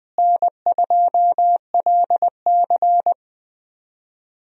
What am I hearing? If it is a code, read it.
Morse code, heard as a keyed tone, about 20 wpm, 700 Hz. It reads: N2LC
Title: USA Call Signs - 4 Characters - Repeated 3x - 20wpm